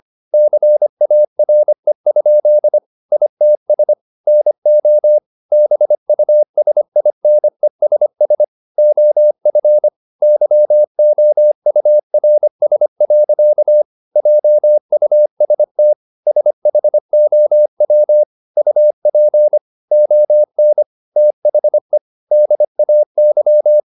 Morse code, heard as a keyed tone, about 25 wpm, 600 Hz. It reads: CARE? ITS NO BUSINESS OF YOURS. JUST SHOW UP ON THE DAY